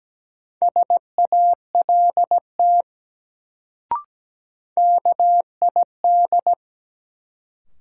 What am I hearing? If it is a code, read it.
Morse code, heard as a keyed tone, about 17 wpm, 700 Hz. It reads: SALT KID
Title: Sets of 5 Words - New Word Drill - 17wpm